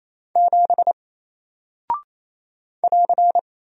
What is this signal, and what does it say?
Morse code, heard as a keyed tone, about 28 wpm, 700 Hz. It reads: 7 <AR>
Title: Single Character (Letter:Number:Punctuation:Prosign) Alt 28wpm